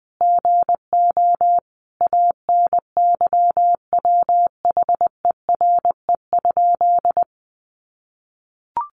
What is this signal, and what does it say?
Morse code, heard as a keyed tone, about 19 wpm, 700 Hz. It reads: GO ANYWHERE?
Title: Sentences from Top 1400 Words 20wpm